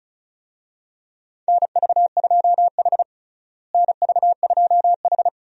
Morse code, heard as a keyed tone, about 34 wpm, 700 Hz. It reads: NV2H NV2H
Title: USA Call Signs - 4 Characters - Repeated 2x - 35wpm